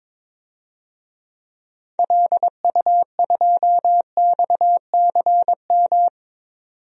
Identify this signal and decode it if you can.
Morse code, heard as a keyed tone, about 22 wpm, 700 Hz. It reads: LU2XCM
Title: DXCC-100 Call Signs - 6 Characters - 22wpm